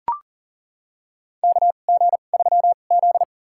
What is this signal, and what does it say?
Morse code, heard as a keyed tone, about 40 wpm, spaced as for 29 wpm, 700 Hz. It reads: KG3Z
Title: USA Call Signs - 4 Characters - 2x Character Spacing - 40wpm